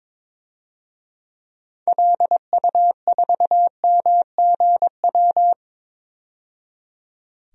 Morse code, heard as a keyed tone, about 22 wpm, 700 Hz. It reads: LU4MGW
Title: DXCC-100 Call Signs - 6 Characters - 22wpm